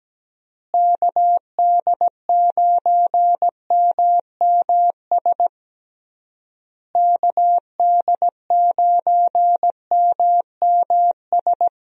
Morse code, heard as a keyed tone, about 17 wpm, 700 Hz. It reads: KD9MMS KD9MMS
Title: USA Call Signs - 6 Characters - Repeated 3x - 17wpm